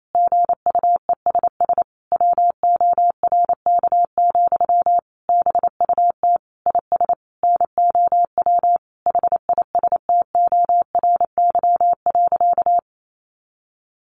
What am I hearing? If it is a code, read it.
Morse code, heard as a keyed tone, about 28 wpm, 700 Hz. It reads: GUESS WORK, BUT IS NOW HISTORY.
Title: Sentences from Top 1500 Words 28wpm